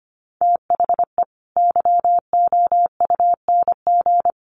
Morse code, heard as a keyed tone, about 25 wpm, 700 Hz. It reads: THE YOUNG
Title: Sentences from Top 1100 Words 25wpm